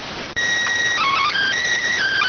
SSTV (slow-scan television), PD 290.